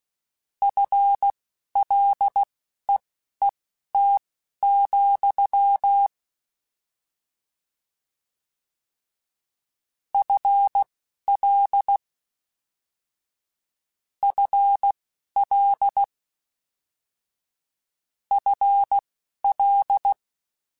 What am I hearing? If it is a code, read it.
Morse code, heard as a keyed tone, about 16 wpm, spaced as for 12 wpm, 800 Hz. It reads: FLEET, FL FL FL